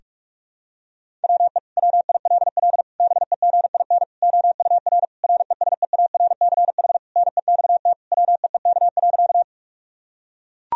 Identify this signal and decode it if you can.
Morse code, heard as a keyed tone, about 42 wpm, 700 Hz. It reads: WE WILL BEGIN OUR RESEARCH NEXT WEEK.
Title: Sentences from Top 1000 Words - Review Part 4 - 45wpm